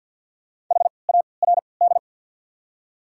Morse code, heard as a keyed tone, about 54 wpm, spaced as for 29 wpm, 690 Hz. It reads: HARD
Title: 3x Character Spacing - Top 300 Words - 50wpm